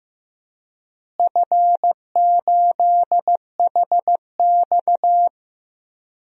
Morse code, heard as a keyed tone, about 15 wpm, 700 Hz. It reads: F8HX
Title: DXCC-100 Call Signs - 4 Characters - Repeated 2x - 15wpm